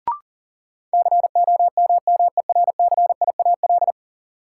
Morse code, heard as a keyed tone, about 40 wpm, 700 Hz. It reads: COMMERCIAL
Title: Sentences from Top 1000 Words - Review Part 5 - New Word Drill - 40wpm